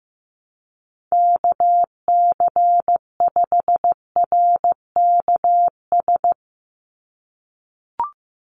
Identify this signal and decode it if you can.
Morse code, heard as a keyed tone, about 15 wpm, 700 Hz. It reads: KC5RKS
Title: USA Call Signs 15wpm